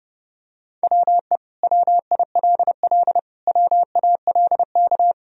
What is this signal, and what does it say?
Morse code, heard as a keyed tone, about 32 wpm, 700 Hz. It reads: WE WILL WALK
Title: Sentences from Top 1000 Words - Review Part 1 - 30wpm